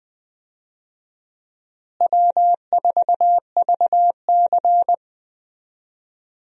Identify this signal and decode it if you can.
Morse code, heard as a keyed tone, about 21 wpm, 700 Hz. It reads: W4VC
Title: USA Call Signs - 4 Characters - Repeated 3x - 20wpm